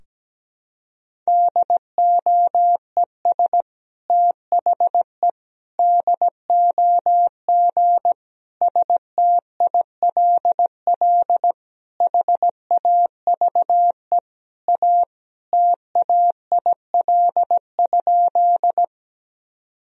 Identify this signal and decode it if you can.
Morse code, heard as a keyed tone, about 17 wpm, 700 Hz. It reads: DOES THE DOG STILL HAVE A TAIL?
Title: Sentences from Top 1000 Words - Review Part 3 - 17wpm